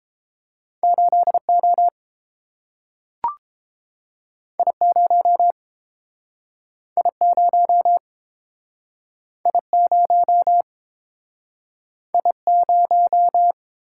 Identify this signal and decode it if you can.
Morse code, heard as a keyed tone, about 30 wpm, 700 Hz. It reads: 8O I0 I0 I0 I0